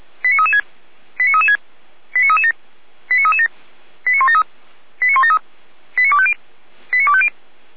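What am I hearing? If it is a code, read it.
Five-tone selective call (ZVEI), heard as quick runs of five tones, tones of about 70 ms. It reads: 89336 89336 89337 89337 89173 89173 89250 89250
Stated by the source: ZVEI five-tone selective call received on SDR, from the Feuerwehr Orlamuende audio archive.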